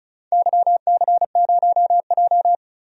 Morse code, heard as a keyed tone, about 34 wpm, 700 Hz. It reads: YC0J
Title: DXCC-100 Call Signs - 4 Characters - 35wpm